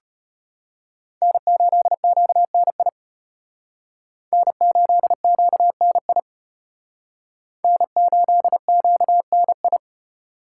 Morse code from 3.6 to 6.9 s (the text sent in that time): N8QNI